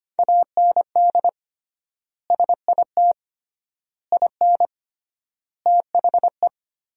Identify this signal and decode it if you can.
Morse code, heard as a keyed tone, about 25 wpm, 700 Hz. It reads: AND SIT IN THE
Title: 3x Word-Spacing - Sentences from Top 500 Words - 25wpm